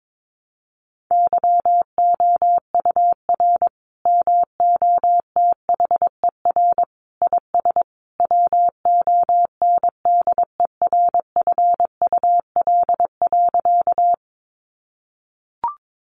Morse code, heard as a keyed tone, about 22 wpm, 700 Hz. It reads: YOUR MOTHER IS WONDERFUL.
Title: Sentences from Top 1100 Words 22wpm